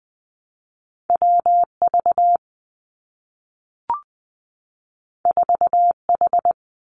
Morse code, heard as a keyed tone, about 20 wpm, 700 Hz. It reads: W V 4 H
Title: Two Challenging Letter-Number Combinations 20wpm